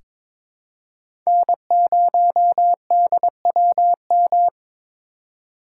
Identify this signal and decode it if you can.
Morse code, heard as a keyed tone, about 22 wpm, 700 Hz. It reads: N0DWM
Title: USA Call Signs - 5 Characters - 4x Character Spacing - 22wpm